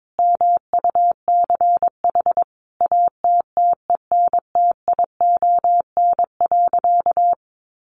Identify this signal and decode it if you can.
Morse code, heard as a keyed tone, about 22 wpm, 700 Hz. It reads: MUCH ATTENTION.
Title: Sentences from Top 1000 Words 22wpm